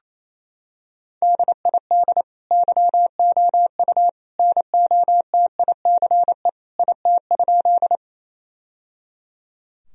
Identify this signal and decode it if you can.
Morse code, heard as a keyed tone, about 28 wpm, 700 Hz. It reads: DID YOU NOTICE IT?
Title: Sentences from Top 1000 Words - Review Part 2 - 28wpm